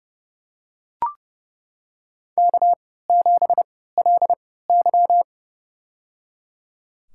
Morse code, heard as a keyed tone, about 28 wpm, spaced as for 17 wpm, 700 Hz. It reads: K7LY
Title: USA Call Signs - 4 Characters - 3x Character Spacing - 30wpm